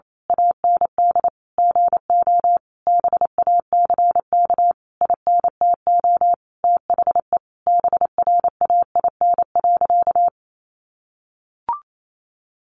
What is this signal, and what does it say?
Morse code, heard as a keyed tone, about 28 wpm, 700 Hz. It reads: AND GO BACK INTO THE BRAIN.
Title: A Sound Of Thunder 28wpm